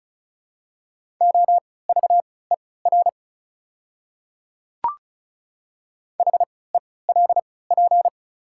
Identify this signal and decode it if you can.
Morse code, heard as a keyed tone, about 34 wpm, spaced as for 20 wpm, 700 Hz. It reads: OVER HELP
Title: 3x Character Spacing - Top 300 Words - 35wpm